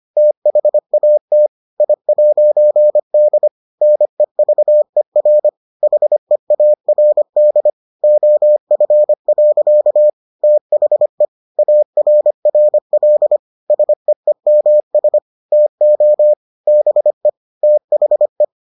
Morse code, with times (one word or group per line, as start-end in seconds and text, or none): 0.2-1.5 THAT
1.8-3.5 I'D
3.8-5.5 NEVER
5.8-7.7 HEARD
8.0-10.1 OF.
10.4-11.2 THE
11.6-13.4 ARRL
13.7-15.2 SEEMS
15.5-16.3 TO
16.7-17.3 BE
17.6-18.4 THE